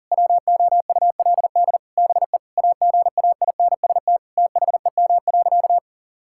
Morse code, heard as a keyed tone, about 40 wpm, 700 Hz. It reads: WOULD BE AGAINST THEM.
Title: Sentences from Top 1000 Words - Review Part 6 - 40wpm